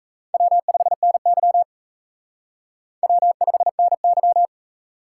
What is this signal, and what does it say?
Morse code, heard as a keyed tone, about 34 wpm, 700 Hz. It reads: W5NY W5NY
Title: Speed Racing - USA Call Signs 28wpm